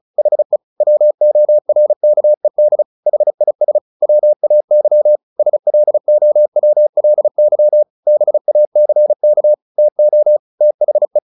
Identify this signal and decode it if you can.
Morse code, heard as a keyed tone, about 35 wpm, 600 Hz. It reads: HE WORKED HIS WAY SLOWLY BACK TO THE
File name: the old man and the sea 2 35wpm